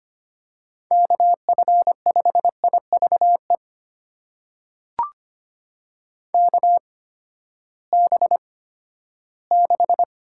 Morse code, heard as a keyed tone, about 25 wpm, spaced as for 14 wpm, 700 Hz. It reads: KF5IVE KB6
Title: USA Call Signs - 6 Characters - 8x Character Spacing - 25wpm